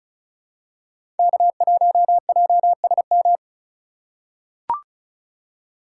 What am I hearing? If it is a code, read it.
Morse code, heard as a keyed tone, about 34 wpm, 700 Hz. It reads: K1JSM
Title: USA Call Signs - 5 Characters - 3x Character Spacing - 35wpm